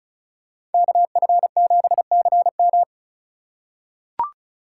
Morse code, heard as a keyed tone, about 36 wpm, 700 Hz. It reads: KF7CM
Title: USA Call Signs - 5 Characters - 4x Character Spacing - 35wpm